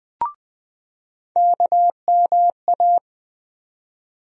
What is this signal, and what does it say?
Morse code, heard as a keyed tone, about 20 wpm, 700 Hz. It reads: KMA
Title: USA Call Sign Postfixes - 3 Letters 20wpm